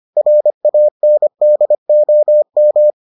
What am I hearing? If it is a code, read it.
Morse code, heard as a keyed tone, about 25 wpm, 600 Hz. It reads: RANDOM